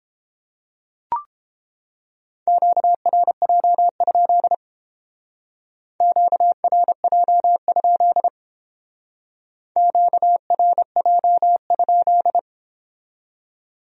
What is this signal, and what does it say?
Morse code, heard as a keyed tone, about 30 wpm, 700 Hz. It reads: QRJ? QRJ? QRJ?